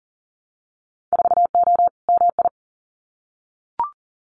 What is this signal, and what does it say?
Morse code, heard as a keyed tone, about 39 wpm, 700 Hz. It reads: 4O MI
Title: Winter Field Day Exchanges Encore 40wpm